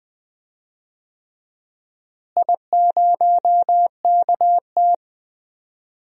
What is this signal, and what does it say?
Morse code, heard as a keyed tone, about 21 wpm, 700 Hz. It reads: I0KT
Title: DXCC-100 Call Signs - 4 Characters - Repeated 3x - 20wpm